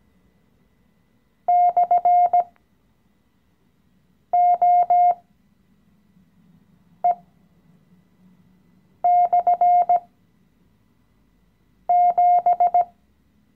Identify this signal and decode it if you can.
Morse code, heard as a keyed tone, about 17 wpm, spaced as for 4 wpm, 710 Hz. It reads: /OE/7
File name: CW-Online-Kurs / Anfänger / Lektion 016